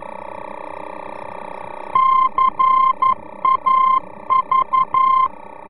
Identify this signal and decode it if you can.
Morse code, heard as a keyed tone, about 11 wpm, 1050 Hz. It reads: CAV
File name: france / mx / CAV-VOR-F